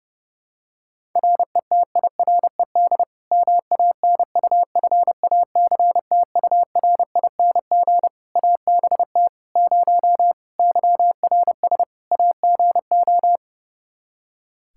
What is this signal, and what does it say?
Morse code, heard as a keyed tone, about 30 wpm, 700 Hz. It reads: RETIRED MANUFACTURING ABT 0 YRS AGO
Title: Practice QSO - Exchange 3 - More Info - 30wpm